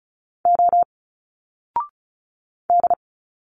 Morse code, heard as a keyed tone, about 34 wpm, 700 Hz. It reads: O D
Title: Single Letters 35wpm